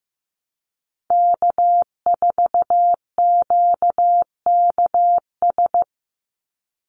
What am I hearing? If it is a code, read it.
Morse code, heard as a keyed tone, about 15 wpm, 700 Hz. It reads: K4QKS
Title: USA Call Signs 15wpm